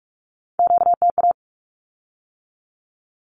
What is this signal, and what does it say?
Morse code, heard as a keyed tone, about 46 wpm, 700 Hz. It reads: QTA
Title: Q-Codes with Spoken Meaning - 45wpm